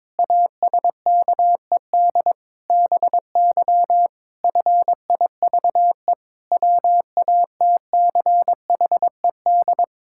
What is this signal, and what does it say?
Morse code, heard as a keyed tone, about 22 wpm, 700 Hz. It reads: ASKED BY FIVE WATCHED
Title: Pangram Challenge v2 - 22wpm